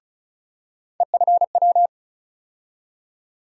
Morse code, heard as a keyed tone, about 36 wpm, 700 Hz. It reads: EFW
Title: USA Call Sign Postfixes - 3 Letters 35wpm